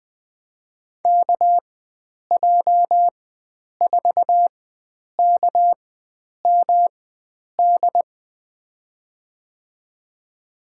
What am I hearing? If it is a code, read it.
Morse code, heard as a keyed tone, about 20 wpm, spaced as for 9 wpm, 700 Hz. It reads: KJ4KMD
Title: USA Call Signs - 4x Character Spacing - 20wpm